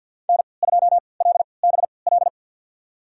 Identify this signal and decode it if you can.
Morse code, heard as a keyed tone, about 52 wpm, spaced as for 28 wpm, 700 Hz. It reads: N2LBL
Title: USA Call Signs - 5 Characters - 3x Character Spacing - 50wpm